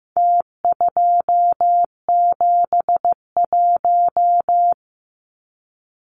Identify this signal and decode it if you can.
Morse code, heard as a keyed tone, about 15 wpm, 700 Hz. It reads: T271
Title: Numbers - Contest Style - 4-Digit 15wpm